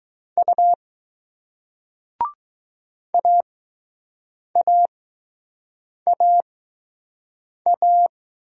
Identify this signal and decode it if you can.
Morse code, heard as a keyed tone, about 20 wpm, spaced as for 6 wpm, 700 Hz. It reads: U AAAA